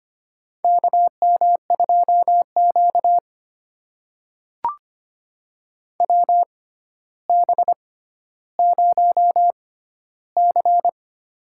Morse code, heard as a keyed tone, about 25 wpm, spaced as for 13 wpm, 700 Hz. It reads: KM2Q WB0C
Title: USA Call Signs - 4 Characters - 6x Character Spacing - 25wpm